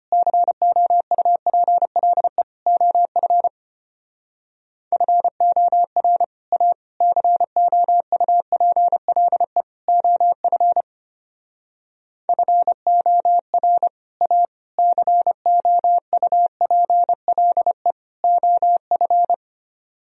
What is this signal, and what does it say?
Morse code, heard as a keyed tone, about 31 wpm, 700 Hz. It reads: COUPLE OF FOR A COUPLE OF FOR A COUPLE OF
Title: Speed-Racing - Sets of 4 Words 25wpm